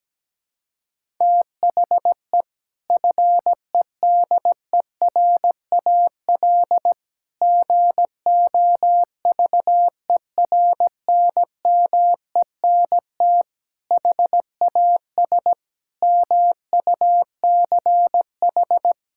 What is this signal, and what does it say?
Morse code, heard as a keyed tone, about 17 wpm, 700 Hz. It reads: THE FEDERAL GOVERNMENT HAS MUCH
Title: Sentences from Top 1000 Words - Review Part 5 - 17wpm